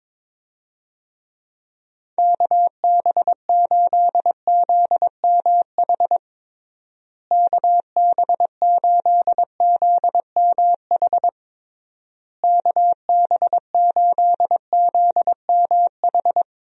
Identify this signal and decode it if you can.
Morse code, heard as a keyed tone, about 22 wpm, 700 Hz. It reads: KB8ZMH KB8ZMH KB8ZMH
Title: USA Call Signs - 6 Characters - Repeated 3x - 22wpm